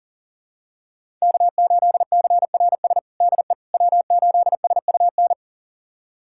Morse code, heard as a keyed tone, about 40 wpm, 700 Hz. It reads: K8CRS DE W8SUN
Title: USA Callsign DE Callsign - 5 Characters - 40wpm